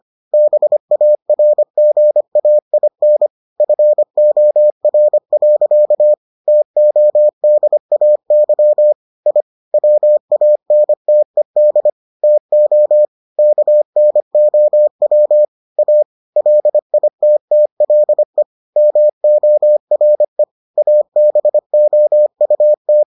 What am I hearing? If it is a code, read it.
Morse code, heard as a keyed tone, about 25 wpm, 600 Hz. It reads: BARGAIN FOR. TODAY I WANTED TO KNOW A LITTLE MORE ABOUT